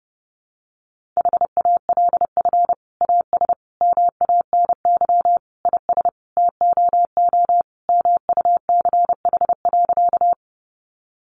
Morse code, heard as a keyed tone, about 30 wpm, 700 Hz. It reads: HALF AS MANY IS TOO MUCH.
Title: Sentences from Top 400 Words 30wpm